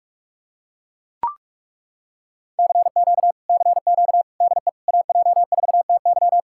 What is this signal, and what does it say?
Morse code, heard as a keyed tone, about 40 wpm, 700 Hz. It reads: CQ CQ DE AJ4TY